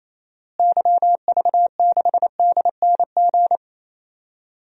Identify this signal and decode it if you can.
Morse code, heard as a keyed tone, about 28 wpm, 700 Hz. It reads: YV6DNG
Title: DXCC-100 Call Signs - 6 Characters - Repeated 2x - 28wpm